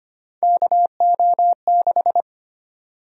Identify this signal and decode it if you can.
Morse code, heard as a keyed tone, about 25 wpm, 700 Hz. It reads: KO6
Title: USA Call Sign Prefixes - 25wpm